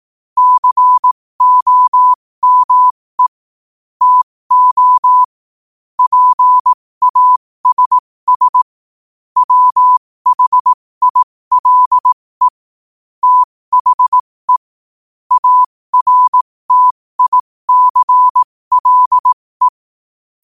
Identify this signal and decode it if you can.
Morse code, heard as a keyed tone, about 18 wpm, spaced as for 15 wpm, 1000 Hz. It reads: COME TO PASS WHILE THE ARTICLE
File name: qotdfiles / 15wpm